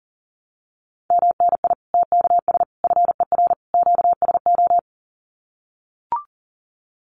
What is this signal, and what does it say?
Morse code, heard as a keyed tone, about 40 wpm, 700 Hz. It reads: MNI TKS FER QSO
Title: Common QSO Phrases 40wpm